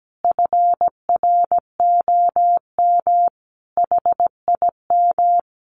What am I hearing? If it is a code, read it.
Morse code, heard as a keyed tone, about 17 wpm, 700 Hz. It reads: FROM HIM
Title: Sentences from Top 1300 Words 17wpm